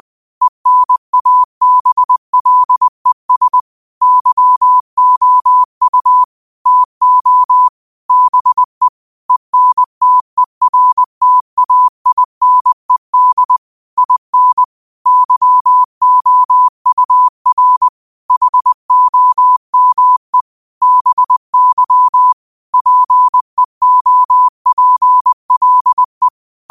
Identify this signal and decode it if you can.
Morse code, heard as a keyed tone, about 20 wpm, 1000 Hz. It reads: ENABLES YOU TO BE ENTERTAINED IN YOUR HOME BY PEOPLE
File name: qotdfiles / 20wpm